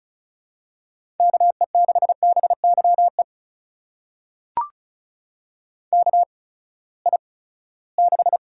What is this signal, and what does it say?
Morse code, heard as a keyed tone, about 36 wpm, spaced as for 20 wpm, 700 Hz. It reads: KE6BYE KI6
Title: USA Call Signs - 6 Characters - 8x Character Spacing - 35wpm